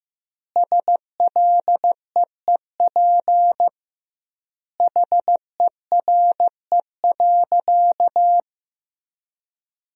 Morse code, heard as a keyed tone, about 16 wpm, 700 Hz. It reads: SLEEP HERE.
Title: 2x Word-Spacing - Sentences from Top 500 Words - 15wpm